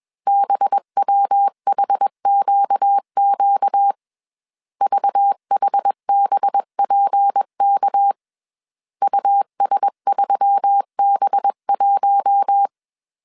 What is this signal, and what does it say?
Morse code, heard as a keyed tone, about 21 wpm, 800 Hz. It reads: BWHQQ VHBPK US3B1
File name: CW / DET / 1 6 1500C S B 20wpm 2048